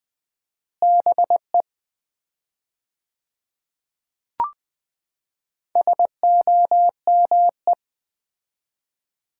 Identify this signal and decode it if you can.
Morse code, heard as a keyed tone, about 20 wpm, 700 Hz. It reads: BE SOME
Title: 4x Word-Spacing - Sets of 3 Words - 20wpm